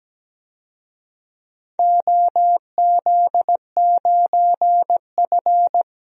Morse code, heard as a keyed tone, about 17 wpm, 700 Hz. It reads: OZ9F
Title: DXCC-100 Call Signs - 4 Characters - 17wpm